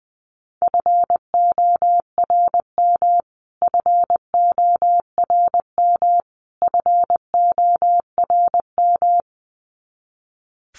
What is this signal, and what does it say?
Morse code, heard as a keyed tone, about 19 wpm, 700 Hz. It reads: FORM FORM FORM
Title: Top 500 Words Repeated 20wpm